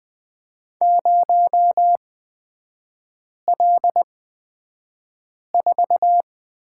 Morse code, heard as a keyed tone, about 20 wpm, spaced as for 5 wpm, 690 Hz. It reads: 0L4
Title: 229 Lesson 37e - Letter x - Three Characters 20wpm